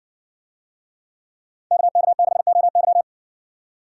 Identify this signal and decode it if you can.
Morse code, heard as a keyed tone, about 56 wpm, 700 Hz. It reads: KK6OY